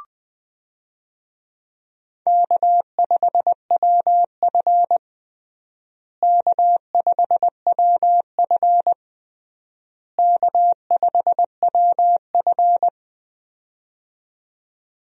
Morse code, heard as a keyed tone, about 21 wpm, 700 Hz. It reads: K5WF K5WF K5WF
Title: USA Call Signs - 4 Characters - Repeated 3x - 20wpm